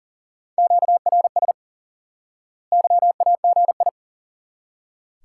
Morse code, heard as a keyed tone, about 39 wpm, 700 Hz. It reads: QRS YAGI
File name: course / instant-qso-element-course-3 / IQE Course 3 - Lesson 89 - YAGI - Two Elements - 40wpm